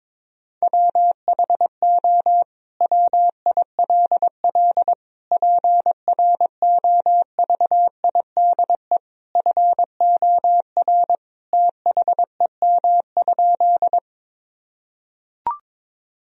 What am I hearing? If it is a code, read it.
Morse code, heard as a keyed tone, about 22 wpm, 700 Hz. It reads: WHO WILL PROVIDE FOR THEM?
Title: Sentences from Top 1000 Words - Review Part 3 - 22wpm